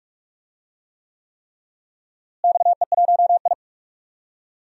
Morse code, heard as a keyed tone, about 42 wpm, 700 Hz. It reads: XE1I
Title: DXCC-100 Call Signs - 4 Characters - Repeated 3x - 45wpm